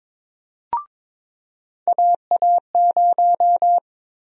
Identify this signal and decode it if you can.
Morse code, heard as a keyed tone, about 22 wpm, 700 Hz. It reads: AA0
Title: USA Call Sign Prefixes - Repeated 2x - 22wpm